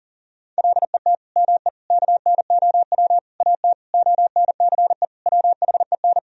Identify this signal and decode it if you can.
Morse code, heard as a keyed tone, about 40 wpm, 700 Hz. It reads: LET ME KNOW AT ONCE WHEN
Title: Sentences from Top 300 Words and Abbreviations 40wpm